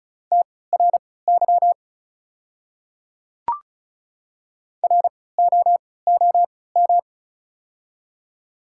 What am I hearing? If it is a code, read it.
Morse code, heard as a keyed tone, about 34 wpm, spaced as for 20 wpm, 690 Hz. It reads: TRY ROOM
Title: 3x Character Spacing - Top 500 Words - 35wpm